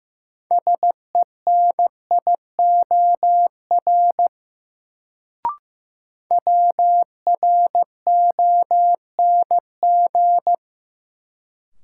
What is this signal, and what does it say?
Morse code, heard as a keyed tone, about 15 wpm, 700 Hz. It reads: SENIOR WRONG
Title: Sets of 5 Words Encore - New Word Drill - 15wpm